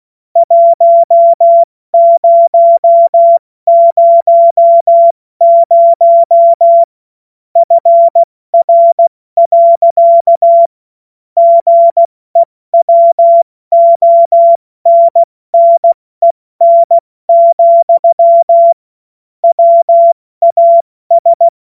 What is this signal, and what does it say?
Morse code, heard as a keyed tone, about 16 wpm, spaced as for 14 wpm, 680 Hz. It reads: 1000 FR. GEWONNEN, WAS
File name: Morsekurs / Witze / Witz 26. Oktober 2021, 70 F80